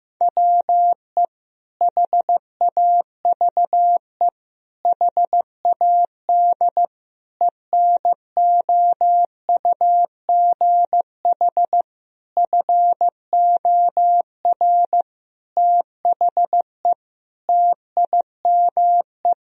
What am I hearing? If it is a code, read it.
Morse code, heard as a keyed tone, about 15 wpm, 700 Hz. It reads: WE HAVE HAD ENOUGH FOR THE TIME
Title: Sentences from Top 1000 Words - Review Part 3 - 15wpm